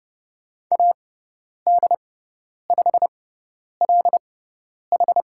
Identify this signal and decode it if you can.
Morse code, heard as a keyed tone, about 30 wpm, spaced as for 10 wpm, 700 Hz. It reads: AD5LH
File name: Lesson 16g - Letter c - Call Signs 30wpm